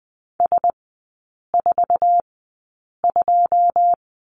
Morse code, heard as a keyed tone, about 20 wpm, spaced as for 8 wpm, 700 Hz. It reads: S42
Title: 081 Lesson 14e - Number 2 - Three Characters 20wpm